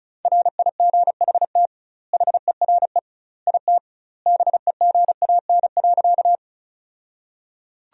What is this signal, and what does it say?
Morse code, heard as a keyed tone, about 36 wpm, 700 Hz. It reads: RIGHT HERE IT BEGAN.
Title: 2x Word-Spacing - Sentences from Top 300 Words - 35wpm